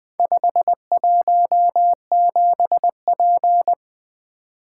Morse code, heard as a keyed tone, about 20 wpm, 700 Hz. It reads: 517P
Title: Four Letter-Number Combinations - 20wpm